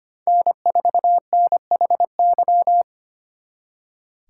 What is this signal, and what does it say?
Morse code, heard as a keyed tone, about 33 wpm, 700 Hz. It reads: N4NHY